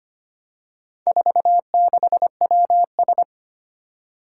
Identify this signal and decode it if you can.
Morse code, heard as a keyed tone, about 25 wpm, 700 Hz. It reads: 46WS
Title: Four Challenging Letter-Number Combinations 25wpm